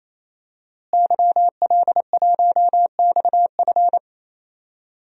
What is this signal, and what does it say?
Morse code, heard as a keyed tone, about 28 wpm, 700 Hz. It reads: YL1XF
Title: DXCC-100 Call Signs - 5 Characters - Repeated 3x - 28wpm